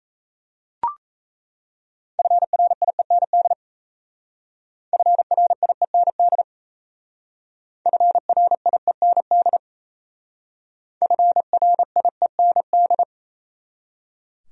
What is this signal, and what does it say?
Morse code, heard as a keyed tone, about 37 wpm, 690 Hz. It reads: FRIEND FRIEND FRIEND FRIEND